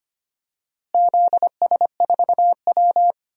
Morse code, heard as a keyed tone, about 25 wpm, 700 Hz. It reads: ZS4W
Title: DXCC-100 Call Signs - 4 Characters - Repeated 2x - 25wpm